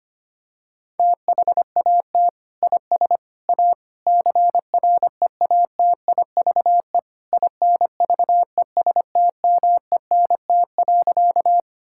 Morse code, heard as a keyed tone, about 25 wpm, 700 Hz. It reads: THAT IS A CREATIVE INVESTMENT.
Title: Sentences from Top 1000 Words - Review Part 5 - 25wpm